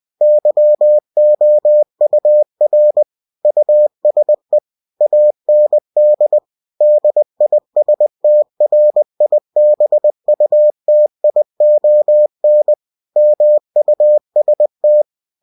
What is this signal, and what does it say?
Morse code, heard as a keyed tone, about 20 wpm, 600 Hz. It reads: YOUR USE AND DISTRIBUTION MUST
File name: cw-books / winnie-the-pooh / winnie-20wpm-page-0115